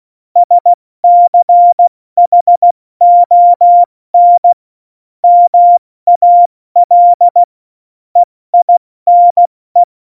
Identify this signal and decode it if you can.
Morse code, heard as a keyed tone, about 16 wpm, 700 Hz. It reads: SCHON MAL EINE
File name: Morsekurs / Witze / Witz 03. Oktober 17, Tempo 70 F80